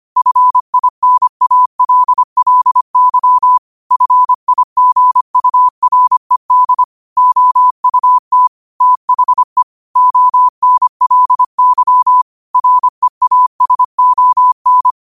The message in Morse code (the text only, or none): FINALLY FIGURED OUT THE ONLY REASON